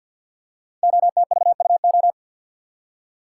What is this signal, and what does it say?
Morse code, heard as a keyed tone, about 53 wpm, 700 Hz. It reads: OTVUO
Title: Code Groups - 5 Letter:Numbers - 50wpm